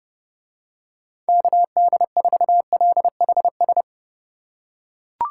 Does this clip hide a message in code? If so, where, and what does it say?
Yes: Morse code from 1.3 to 3.8 s, heard as a keyed tone, about 30 wpm, 700 Hz. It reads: KD4LHS